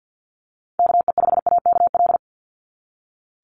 Morse code, heard as a keyed tone, about 48 wpm, 700 Hz. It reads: KE5AKL
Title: NA SOTA Chaser Callsigns 50wpm